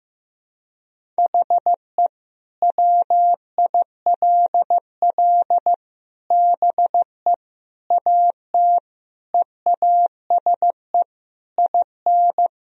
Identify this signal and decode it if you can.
Morse code, heard as a keyed tone, about 15 wpm, 700 Hz. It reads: HE WILL BE AT EASE IN
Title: Sentences from Top 1000 Words - Review Part 1 - 15wpm